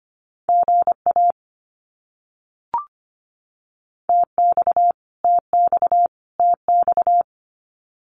Morse code, heard as a keyed tone, about 25 wpm, 700 Hz. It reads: GA TX TX TX
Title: Abbreviations Repeated 3x - 25wpm